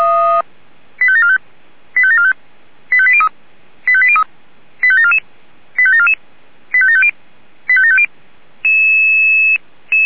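Five-tone selective call (ZVEI), heard as quick runs of five tones, tones of about 70 ms. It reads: 85745 85745 85803 85803 85744 85744 85750 85750
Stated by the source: Received on an SDR, ZVEI five-tone code 85730.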